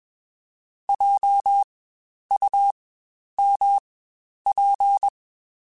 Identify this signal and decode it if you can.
Morse code, heard as a keyed tone, about 21 wpm, spaced as for 10 wpm, 800 Hz. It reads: JUMP
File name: Spaced Alphas / Spaced Alphas mp3s / Js sp @20